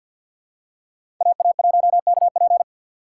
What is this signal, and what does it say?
Morse code, heard as a keyed tone, about 53 wpm, 690 Hz. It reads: AA1KP
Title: USA Call Signs - 5 Characters - 6x Character Spacing - 50wpm